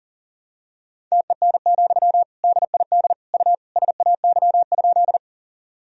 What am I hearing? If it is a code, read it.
Morse code, heard as a keyed tone, about 37 wpm, 700 Hz. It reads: TEN, DID U SAY?
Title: Sentences from Top 400 Words and Abbreviations 40wpm